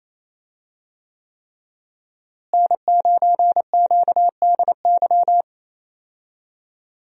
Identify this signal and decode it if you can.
Morse code, heard as a keyed tone, about 28 wpm, 700 Hz. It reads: N9QDY
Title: USA Call Signs - 5 Characters - Repeated 3x - 28wpm